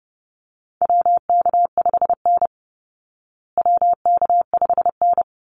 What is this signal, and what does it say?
Morse code, heard as a keyed tone, about 30 wpm, 700 Hz. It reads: WK5N WK5N
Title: Call Signs Repeated 2x - 30wpm